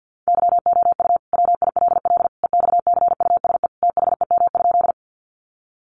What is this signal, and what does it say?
Morse code, heard as a keyed tone, about 50 wpm, 700 Hz. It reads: YOU WILL EXCUSE THEM?
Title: Sentences from Top 700 Words 50wpm